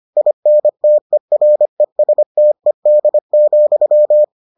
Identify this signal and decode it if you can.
Morse code, heard as a keyed tone, about 25 wpm, 600 Hz. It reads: INTERESTED,